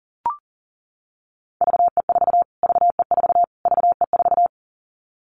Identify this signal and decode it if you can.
Morse code, heard as a keyed tone, about 40 wpm, 700 Hz. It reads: VE4 VE4 VE4
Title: Callsign Prefixes for Top 25 DXCC - Repeated 3x - 40wpm